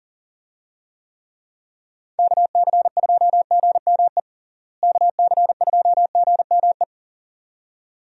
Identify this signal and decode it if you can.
Morse code, heard as a keyed tone, about 40 wpm, 700 Hz. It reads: KC2GME KC2GME
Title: USA Call Signs - 6 Characters - Repeated 2x - 40wpm